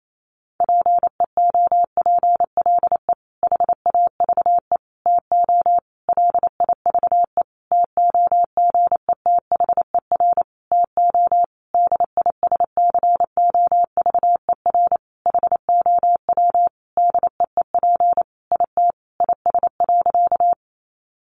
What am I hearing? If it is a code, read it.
Morse code, heard as a keyed tone, about 28 wpm, 700 Hz. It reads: PEOPLE HAVE TO LIVE TOGETHER TO DISCOVER HOW DEEP IT IS.
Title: Sentences from Top 1100 Words 28wpm